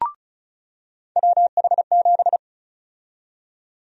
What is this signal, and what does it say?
Morse code, heard as a keyed tone, about 36 wpm, 700 Hz. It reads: WH7
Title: USA Call Sign Prefixes - 35wpm